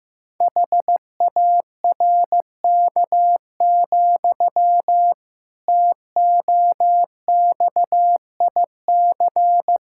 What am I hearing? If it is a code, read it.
Morse code, heard as a keyed tone, about 15 wpm, 700 Hz. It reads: HARK, TOXIC
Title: Pangram Challenge v2 - 15wpm